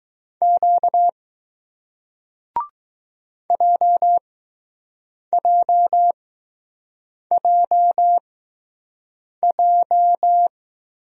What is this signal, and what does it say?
Morse code, heard as a keyed tone, about 20 wpm, spaced as for 6 wpm, 700 Hz. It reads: Q JJJJ